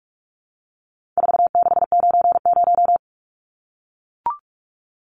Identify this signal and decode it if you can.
Morse code, heard as a keyed tone, about 44 wpm, 700 Hz. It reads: 4690
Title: Numbers - 4-Digit 45wpm